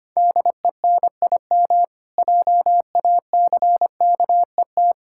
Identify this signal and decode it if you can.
Morse code, heard as a keyed tone, about 25 wpm, 700 Hz. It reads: DENIM JACKET
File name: practice / 25wpm / Difficult Words with Hints - 25wpm